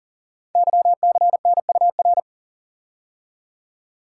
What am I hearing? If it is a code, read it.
Morse code, heard as a keyed tone, about 39 wpm, 700 Hz. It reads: YCNUR
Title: Code Groups - 5 Letter:Numbers - 40wpm